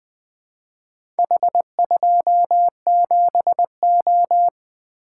Morse code, heard as a keyed tone, about 20 wpm, 700 Hz. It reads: H27O
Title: Four Letter-Number Combinations - 20wpm